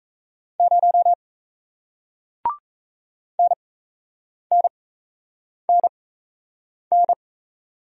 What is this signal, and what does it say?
Morse code, heard as a keyed tone, about 38 wpm, spaced as for 8 wpm, 700 Hz. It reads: 0 NNNN